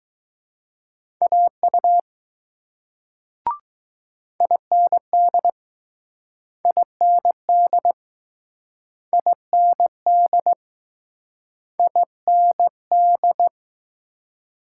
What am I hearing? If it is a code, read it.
Morse code, heard as a keyed tone, about 20 wpm, 700 Hz. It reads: AU IND IND IND IND